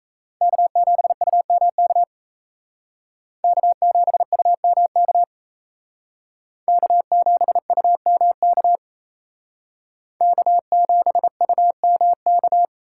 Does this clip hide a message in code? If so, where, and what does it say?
Yes: Morse code from 0.4 to 12.7 s, heard as a keyed tone, about 37 wpm, 700 Hz. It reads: K7UMK K7UMK K7UMK K7UMK